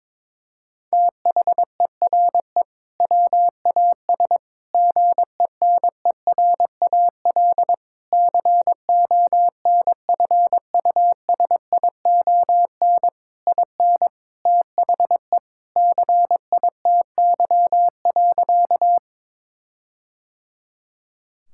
Morse code, heard as a keyed tone, about 22 wpm, 700 Hz. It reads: THERE WAS GENERAL CONFUSION IN THE CITY.
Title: Sentences from Top 1600 Words 22wpm